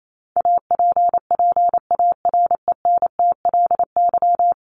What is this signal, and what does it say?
Morse code, heard as a keyed tone, about 28 wpm, 700 Hz. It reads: APPARENTLY
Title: LY Suffix - Top 100 Words with Base Words 28wpm